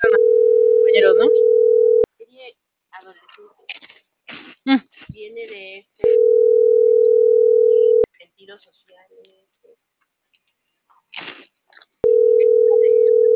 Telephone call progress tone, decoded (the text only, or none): ringback tone